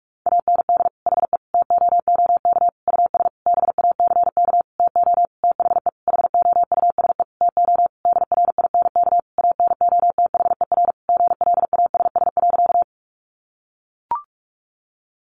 Morse code, heard as a keyed tone, about 44 wpm, 700 Hz. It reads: AND HE TOOK US BACK TO THE HOUSE TO DRINK ANOTHER GLASS.
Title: Sentences from Top 1500 Words 45wpm